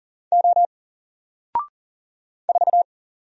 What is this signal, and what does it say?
Morse code, heard as a keyed tone, about 40 wpm, 700 Hz. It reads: O 4
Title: Single Character (Letter:Number:Punctuation:Prosign) 40wpm